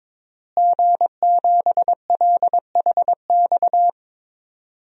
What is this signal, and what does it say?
Morse code, heard as a keyed tone, about 21 wpm, 700 Hz. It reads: G7LHX
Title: DXCC-100 Call Signs - 5 Characters - Repeated 3x - 22wpm